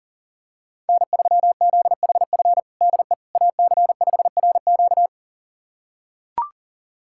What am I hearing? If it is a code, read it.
Morse code, heard as a keyed tone, about 40 wpm, 700 Hz. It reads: N3ZHF DE AC5RQ
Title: USA Callsign DE Callsign - 5 Characters - 40wpm